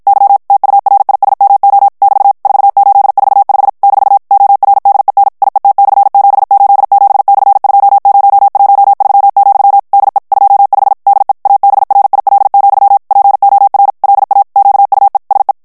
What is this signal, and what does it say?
Morse code, heard as a keyed tone, about 52 wpm, 800 Hz. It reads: Y TURISMO <BT> 4845 - ORDEN IET/787/2013, DE 25 DE ABRIL, POR LA QUE SE
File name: ESPAGNOL TEXTE EN CLAIR 2 VITESSE 50 MOTS MINUTES